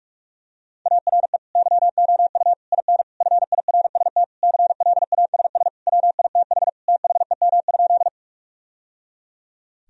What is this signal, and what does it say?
Morse code, heard as a keyed tone, about 46 wpm, 700 Hz. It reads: ARE YOU IN FIRST CLASS WITH THEM?
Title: Sentences from Top 1000 Words - Review Part 2 - 45wpm